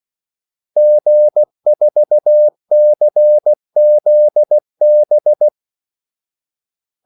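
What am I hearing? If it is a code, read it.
Morse code, heard as a keyed tone, about 16 wpm, 600 Hz. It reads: G4CZB
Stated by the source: Morse call sign practice at 16 wpm.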